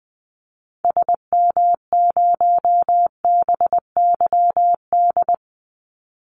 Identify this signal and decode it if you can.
Morse code, heard as a keyed tone, about 20 wpm, 700 Hz. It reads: SM0BYD
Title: Call Signs Repeated 2x - 20wpm